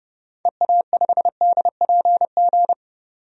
Morse code, heard as a keyed tone, about 32 wpm, 700 Hz. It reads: EA5DPG
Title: DXCC-100 Call Signs - 6 Characters - 30wpm